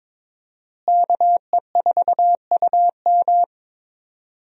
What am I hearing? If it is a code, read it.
Morse code, heard as a keyed tone, about 22 wpm, 700 Hz. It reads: KE4UM
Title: USA Call Signs - 5 Characters - Repeated 2x - 22wpm